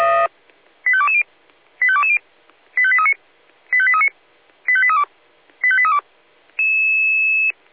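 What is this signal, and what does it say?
Five-tone selective call (ZVEI), heard as quick runs of five tones, tones of about 70 ms. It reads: 85220 85220 85839 85839 85832 85832
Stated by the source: ZVEI five-tone selective call received on SDR.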